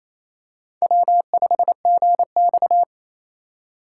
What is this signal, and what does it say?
Morse code, heard as a keyed tone, about 28 wpm, 700 Hz. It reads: W5GX
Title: USA Call Signs - 4 Characters - 3x Character Spacing - 28wpm